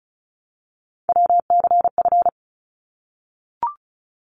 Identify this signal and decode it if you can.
Morse code, heard as a keyed tone, about 36 wpm, 700 Hz. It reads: WCF
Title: ARRL Sections 35wpm